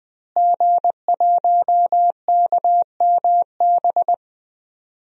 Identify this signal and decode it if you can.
Morse code, heard as a keyed tone, about 21 wpm, 700 Hz. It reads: G1KMB
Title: DXCC-100 Call Signs - 5 Characters - Repeated 2x - 20wpm